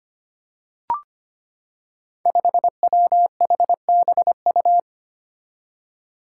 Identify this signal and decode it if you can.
Morse code, heard as a keyed tone, about 25 wpm, 700 Hz. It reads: 5WHBU
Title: Five Challenging Letter-Number Combinations 25wpm